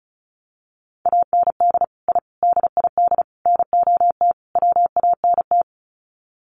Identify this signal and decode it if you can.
Morse code, heard as a keyed tone, about 34 wpm, 700 Hz. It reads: AND I DID NOT WANT
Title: Sets of 5 Words 35wpm